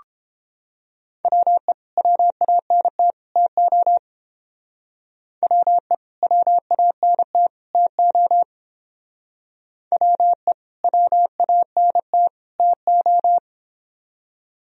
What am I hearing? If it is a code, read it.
Morse code, heard as a keyed tone, about 29 wpm, 700 Hz. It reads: WE WANT TO WE WANT TO WE WANT TO